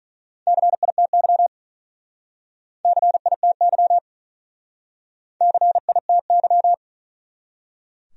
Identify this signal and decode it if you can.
Morse code, heard as a keyed tone, about 47 wpm, 700 Hz. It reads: CITY CITY CITY